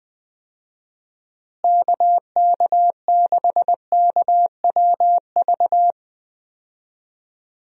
Morse code, heard as a keyed tone, about 20 wpm, 700 Hz. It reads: KK6KWV
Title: USA Call Signs - 6 Characters - Repeated 3x - 20wpm